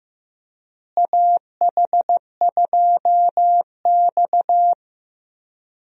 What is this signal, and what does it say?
Morse code, heard as a keyed tone, about 15 wpm, 700 Hz. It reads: AH2X
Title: USA Call Signs - 4 Characters - Repeated 2x - 15wpm